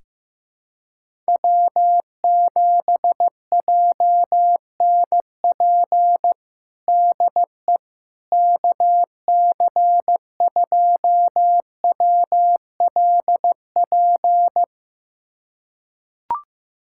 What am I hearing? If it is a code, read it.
Morse code, heard as a keyed tone, about 14 wpm, 700 Hz. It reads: W7JNP DE KC2WLP
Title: USA Callsign DE Callsign 15wpm